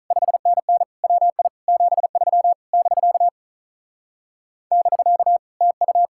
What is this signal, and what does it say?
Morse code, heard as a keyed tone, about 45 wpm, 700 Hz. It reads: 5NN WI 73 <BK> <BK> TU